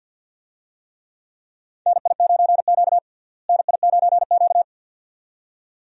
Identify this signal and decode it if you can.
Morse code, heard as a keyed tone, about 53 wpm, 700 Hz. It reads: NI9Q NI9Q
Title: USA Call Signs - 4 Characters - Repeated 2x - 50wpm